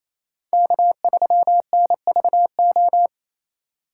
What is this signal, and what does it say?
Morse code, heard as a keyed tone, about 36 wpm, 700 Hz. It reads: K3NVO